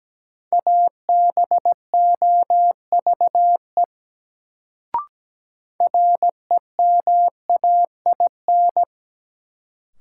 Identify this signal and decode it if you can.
Morse code, heard as a keyed tone, about 17 wpm, 700 Hz. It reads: ABOVE REMAIN
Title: Sentences from Top 1200 Words - New Word Drill Alt - 17wpm